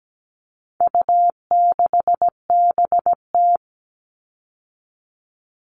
Morse code, heard as a keyed tone, about 17 wpm, 700 Hz. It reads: U6BT
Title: All Cut Numbers - 4-Digit 17wpm